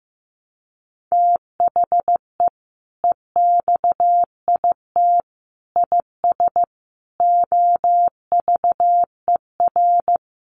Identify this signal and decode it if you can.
Morse code, heard as a keyed tone, about 15 wpm, 700 Hz. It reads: THE EXIT IS OVER
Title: Sentences from Top 1100 Words 15wpm